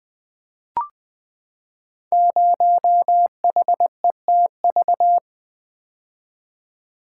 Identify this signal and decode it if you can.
Morse code, heard as a keyed tone, about 20 wpm, 700 Hz. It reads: 0HETV
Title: Code Groups - 5 Letter:Numbers - 20wpm